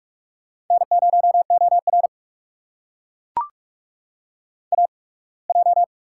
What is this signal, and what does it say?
Morse code, heard as a keyed tone, about 42 wpm, spaced as for 26 wpm, 700 Hz. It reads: N0OR AJ
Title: USA Call Signs - 4 Characters - 8x Character Spacing - 45wpm